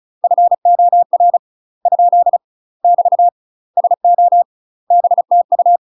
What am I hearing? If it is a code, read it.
Morse code, heard as a keyed tone, about 35 wpm, 700 Hz. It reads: FOR ? <BT> SO BTU
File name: UK MASTER QSOs TRAINING / 35WPM / UK MASTER QSOs 0018